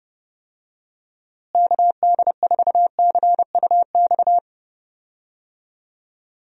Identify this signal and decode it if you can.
Morse code, heard as a keyed tone, about 30 wpm, 700 Hz. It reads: KD4CUX
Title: USA Call Signs - 6 Characters - Repeated 2x - 30wpm